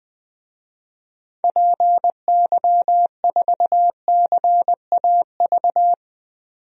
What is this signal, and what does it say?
Morse code, heard as a keyed tone, about 20 wpm, 700 Hz. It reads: PY4CAV
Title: DXCC-100 Call Signs - 6 Characters - 20wpm